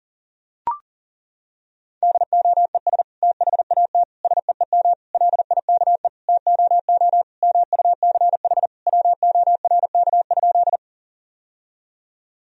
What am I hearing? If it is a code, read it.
Morse code, heard as a keyed tone, about 40 wpm, 700 Hz. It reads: DOES THAT SEEM LIKE TOO MUCH WORK?
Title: Sentences from Top 1000 Words - Review Part 1 - 40wpm